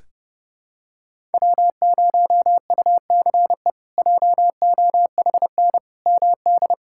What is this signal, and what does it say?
Morse code, heard as a keyed tone, about 31 wpm, 700 Hz. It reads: W0UCE JOHN MD
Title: K1USN SST QSO Practice - 30wpm